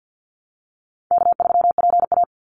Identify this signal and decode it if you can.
Morse code, heard as a keyed tone, about 49 wpm, 700 Hz. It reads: K3PA
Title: Easy Call Signs Repeated 2x - 50wpm